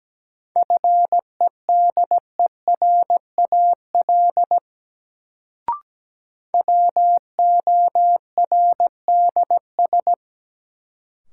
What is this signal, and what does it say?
Morse code, heard as a keyed tone, about 17 wpm, 700 Hz. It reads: FEDERAL WORDS
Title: Sets of 3 Words - New Word Drill - 17wpm